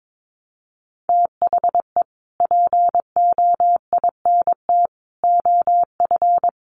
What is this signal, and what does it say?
Morse code, heard as a keyed tone, about 22 wpm, 700 Hz. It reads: THE POINT OF
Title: Sets of 5 Words 22wpm